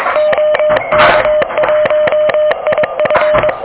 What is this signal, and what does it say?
Morse code, heard as a keyed tone, about 21 wpm, 600 Hz, with noise. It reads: OK1IF
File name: OK QRP 2023 / OK QRP 2023